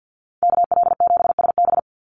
Morse code, heard as a keyed tone, about 50 wpm, 700 Hz. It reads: KL7SB
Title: Call Signs Repeated 2x - 50wpm